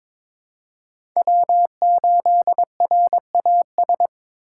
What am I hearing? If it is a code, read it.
Morse code, heard as a keyed tone, about 22 wpm, 700 Hz. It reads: W8RAS
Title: USA Call Signs - 2x Character Spacing - 22wpm